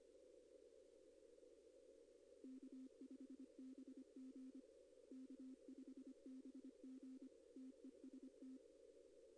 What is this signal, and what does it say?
Morse code, heard as a keyed tone, about 25 wpm, 260 Hz. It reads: K5BG K5BG TEST
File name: contests / arrl-160-k5tr-op-w5tn -2200